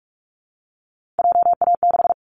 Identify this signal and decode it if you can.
Morse code, heard as a keyed tone, about 46 wpm, 700 Hz. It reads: JA6
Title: Callsign Prefixes for Top 25 DXCC - 45wpm